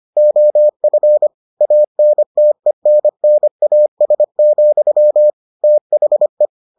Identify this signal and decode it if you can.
Morse code, heard as a keyed tone, about 25 wpm, 600 Hz. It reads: OF ANTENNAS, THE